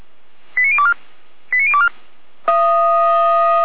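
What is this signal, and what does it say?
Five-tone selective call (ZVEI), heard as quick runs of five tones, tones of about 70 ms. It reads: 89024 89024
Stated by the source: Received on SDR; ZVEI five-tone call, feuerwehr-orlamuende.org (fire brigade Orlamünde).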